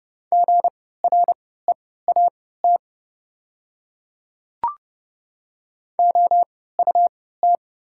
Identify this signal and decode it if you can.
Morse code, heard as a keyed tone, about 30 wpm, spaced as for 17 wpm, 690 Hz. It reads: GREAT OUT
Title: 3x Character Spacing - Top 200 Words - 30wpm